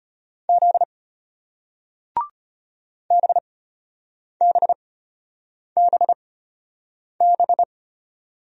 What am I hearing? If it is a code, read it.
Morse code, heard as a keyed tone, about 34 wpm, spaced as for 8 wpm, 700 Hz. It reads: Z BBBB